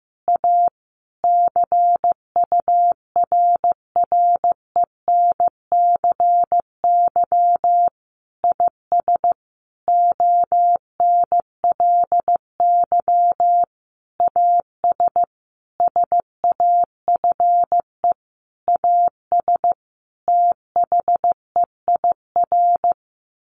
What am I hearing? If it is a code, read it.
Morse code, heard as a keyed tone, about 15 wpm, 700 Hz. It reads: A CURRENCY IS ONLY AS SAFE AS THEIR
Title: Sentences from Top 1500 Words 15wpm